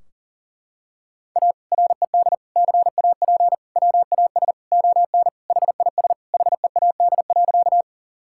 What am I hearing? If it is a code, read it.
Morse code, heard as a keyed tone, about 40 wpm, 700 Hz. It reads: A RED CAP WAS ON HIS HEAD.
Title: Sentences from Top 1600 Words 40wpm